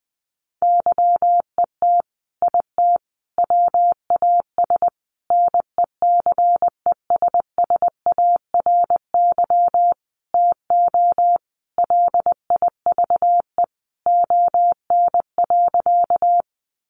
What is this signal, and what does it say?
Morse code, heard as a keyed tone, about 20 wpm, 700 Hz. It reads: YET IT WAS NECESSARY TO LIVE ON.
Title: Sentences from Top 800 Words 20wpm